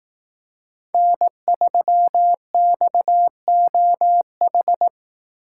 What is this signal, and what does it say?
Morse code, heard as a keyed tone, about 18 wpm, 700 Hz. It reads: N3XOH
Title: Speed Racing - USA 1x3 Call Signs 15wpm